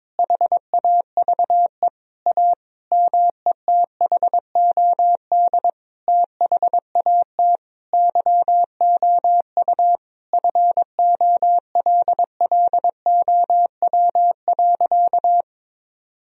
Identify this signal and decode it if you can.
Morse code, heard as a keyed tone, about 22 wpm, 700 Hz. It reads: HAVE A METHOD THAT YOU FOLLOW.
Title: Sentences from Top 1000 Words - Review Part 3 - 22wpm